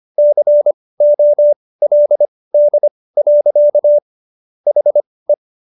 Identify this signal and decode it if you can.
Morse code, heard as a keyed tone, about 25 wpm, spaced as for 18 wpm, 590 Hz. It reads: COLD. HE